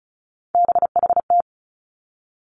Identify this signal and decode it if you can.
Morse code, heard as a keyed tone, about 36 wpm, 700 Hz. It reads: BHT
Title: Three Challenging Letter-Number Combinations 35wpm